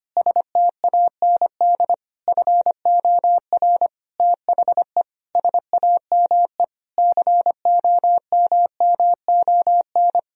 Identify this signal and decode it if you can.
Morse code, heard as a keyed tone, about 25 wpm, 700 Hz. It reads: STAND FOR THE SAME COMMON
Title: Sentences from Top 1000 Words - Review Part 3 - 25wpm